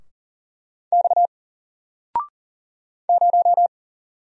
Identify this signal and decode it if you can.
Morse code, heard as a keyed tone, about 41 wpm, 700 Hz. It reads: X 0
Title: Single Letter-Number Alt 40wpm